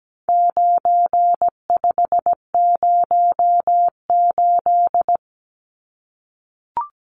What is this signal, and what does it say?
Morse code, heard as a keyed tone, about 17 wpm, 700 Hz. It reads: 9508
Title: Numbers - 4-Digit 17wpm